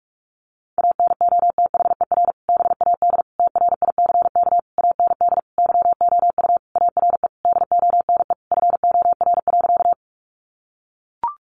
Morse code, heard as a keyed tone, about 44 wpm, 700 Hz. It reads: ANOTHER BAD TRICK AND YOU ARE DONE FOR.
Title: Sentences from Top 1500 Words 45wpm